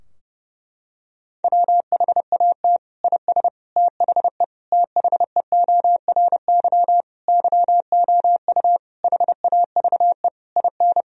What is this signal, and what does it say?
Morse code, heard as a keyed tone, about 30 wpm, 700 Hz. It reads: WHAT IS THE THEORY YOU HAVE IN
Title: Sentences from Top 1000 Words - Review Part 3 - 30wpm